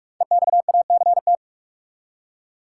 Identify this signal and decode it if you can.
Morse code, heard as a keyed tone, about 46 wpm, 700 Hz. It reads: EXACT
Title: Five Letter Words - 45wpm